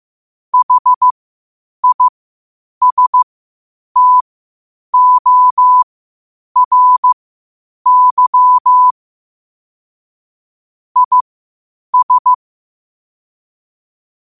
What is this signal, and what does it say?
Morse code, heard as a keyed tone, about 15 wpm, spaced as for 8 wpm, 1000 Hz. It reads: HISTORY IS
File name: qotdfiles / 8wpm